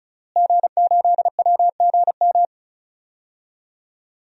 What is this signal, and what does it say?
Morse code, heard as a keyed tone, about 36 wpm, 700 Hz. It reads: G8WGM
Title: DXCC-100 Call Signs - 5 Characters - Repeated 3x - 35wpm